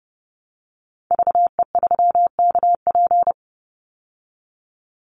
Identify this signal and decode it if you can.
Morse code, heard as a keyed tone, about 30 wpm, 700 Hz. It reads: VE3KP
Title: Call Signs Repeated 3x - 30wpm